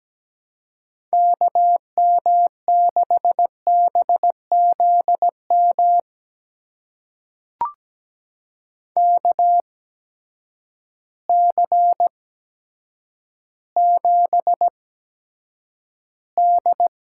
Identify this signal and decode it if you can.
Morse code, heard as a keyed tone, about 17 wpm, spaced as for 8 wpm, 700 Hz. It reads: KM6BZM KC7D
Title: USA Call Signs - 6 Characters - 8x Character Spacing - 17wpm